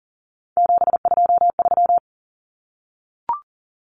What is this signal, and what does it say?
Morse code, heard as a keyed tone, about 40 wpm, 700 Hz. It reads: 723
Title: Numbers - 3-Digit 40wpm